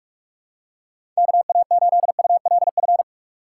Morse code, heard as a keyed tone, about 41 wpm, 700 Hz. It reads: KA8ULF
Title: USA Call Signs - 4x Character Spacing - 45wpm